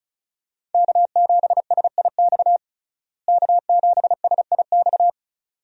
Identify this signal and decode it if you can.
Morse code, heard as a keyed tone, about 34 wpm, 700 Hz. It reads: K7SIX K7SIX
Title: USA Call Signs - 5 Characters - Repeated 2x - 35wpm